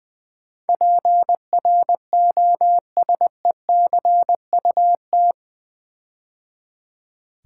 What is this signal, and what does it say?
Morse code, heard as a keyed tone, about 20 wpm, 700 Hz. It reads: PROSECUT
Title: Build A Word - 4 Letters At A Time - 10-Letter Words - 20wpm